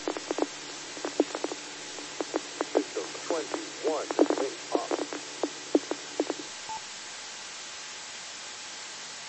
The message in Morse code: none

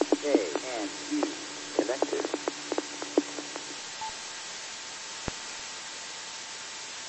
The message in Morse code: none

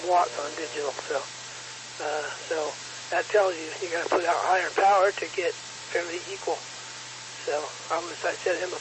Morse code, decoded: none

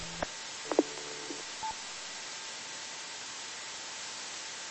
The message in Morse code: none

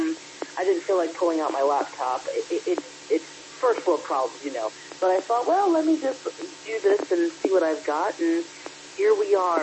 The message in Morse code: none

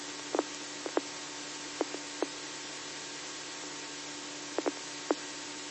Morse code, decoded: none

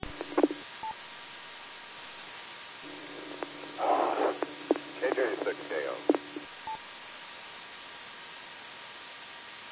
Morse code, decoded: none

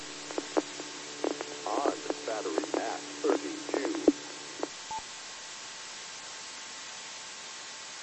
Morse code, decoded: none

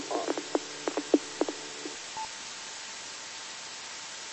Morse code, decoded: none

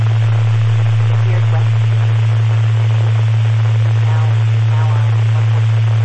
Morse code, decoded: none

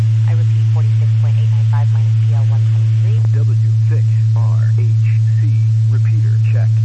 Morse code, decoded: none

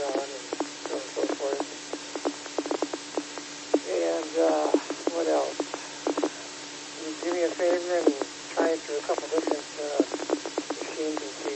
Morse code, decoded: none

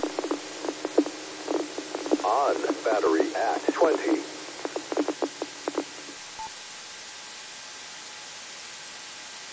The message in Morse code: none

